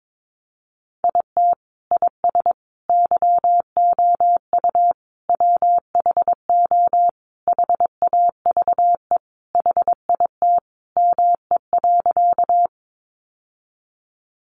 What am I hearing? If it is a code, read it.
Morse code, heard as a keyed tone, about 22 wpm, 700 Hz. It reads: IT IS YOU WHO HAVE HIT ME.
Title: Sentences from Top 1200 Words 22wpm